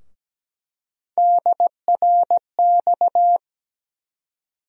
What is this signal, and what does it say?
Morse code, heard as a keyed tone, about 17 wpm, 700 Hz. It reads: DRX
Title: USA Call Sign Postfixes - 3 Letters 17wpm